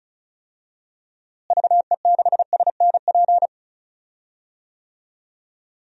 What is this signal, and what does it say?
Morse code, heard as a keyed tone, about 34 wpm, 700 Hz. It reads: VE6SNP
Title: DXCC-100 Call Signs - 6 Characters - Repeated 2x - 35wpm